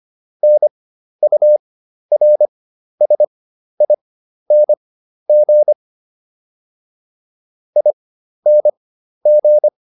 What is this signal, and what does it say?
Morse code, heard as a keyed tone, about 25 wpm, spaced as for 12 wpm, 600 Hz. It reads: NURSING ING